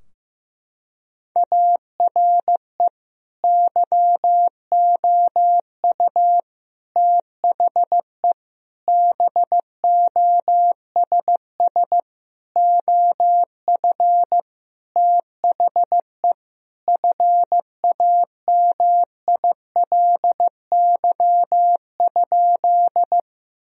Morse code, heard as a keyed tone, about 15 wpm, 700 Hz. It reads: ARE YOU THE BOSS OF THE FAMILY?
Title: Sentences from Top 1000 Words - Review Part 4 - 15wpm